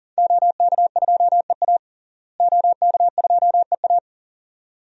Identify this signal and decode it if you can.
Morse code, heard as a keyed tone, about 40 wpm, 700 Hz. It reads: OK2EA OK2EA
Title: DXCC-100 Call Signs - 5 Characters - Repeated 2x - 40wpm